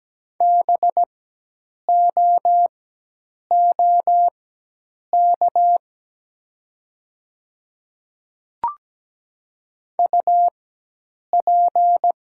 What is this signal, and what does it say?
Morse code, heard as a keyed tone, about 17 wpm, spaced as for 8 wpm, 700 Hz. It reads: BOOK UP
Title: 4x Character Spacing - Top 300 Words - 17wpm